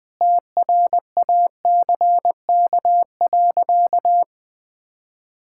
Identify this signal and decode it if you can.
Morse code, heard as a keyed tone, about 20 wpm, 700 Hz. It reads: TRACK.
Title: Sentences with 6-Word Phrases - 20wpm